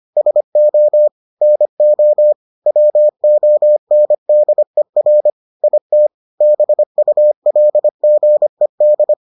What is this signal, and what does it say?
Morse code, heard as a keyed tone, about 25 wpm, 600 Hz. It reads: SO NO WONDER IT BULGED